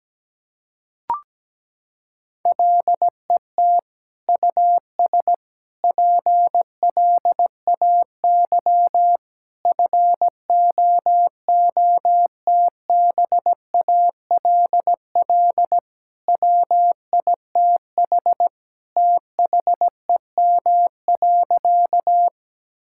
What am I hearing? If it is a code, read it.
Morse code, heard as a keyed tone, about 17 wpm, 700 Hz. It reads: LET US PLAY FOOTBALL WITH THEM.
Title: Sentences from Top 1600 Words 17wpm